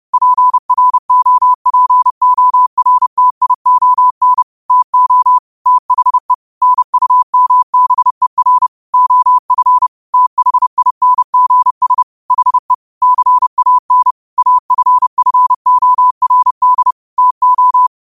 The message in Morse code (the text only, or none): PROPORTION TO THE NUMBER OF THINGS HE CAN AFFORD TO